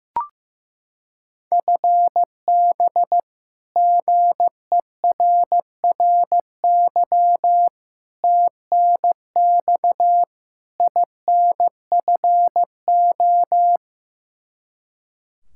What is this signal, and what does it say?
Morse code, heard as a keyed tone, about 15 wpm, 700 Hz. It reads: FB GERRY TNX INFO
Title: Practice QSO - Exchange 3 - More Info - 15wpm